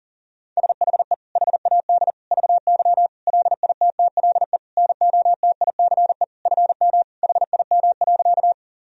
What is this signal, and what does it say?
Morse code, heard as a keyed tone, about 40 wpm, 700 Hz. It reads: SHE HAD VY LITTLE NOTICE FM HIM.
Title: Sentences from Top 400 Words and Abbreviations 40wpm